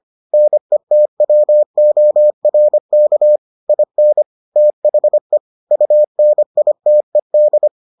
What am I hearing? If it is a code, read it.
Morse code, heard as a keyed tone, about 25 wpm, 600 Hz. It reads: NETWORK IN THE UNITED